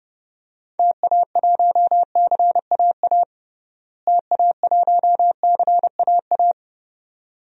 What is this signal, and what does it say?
Morse code, heard as a keyed tone, about 31 wpm, 700 Hz. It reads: TA1CAA TA1CAA
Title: DXCC-100 Call Signs - 6 Characters - Repeated 2x - 30wpm